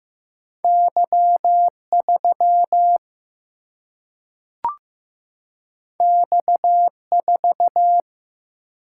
Morse code, heard as a keyed tone, about 15 wpm, 700 Hz. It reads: Y3 X4
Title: Single-Letter Single-Number 15wpm